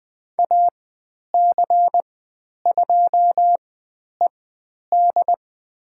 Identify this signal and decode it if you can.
Morse code, heard as a keyed tone, about 20 wpm, spaced as for 10 wpm, 700 Hz. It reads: AC2ED
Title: 102 Lesson 17g - Letter u - Call Signs 20wpm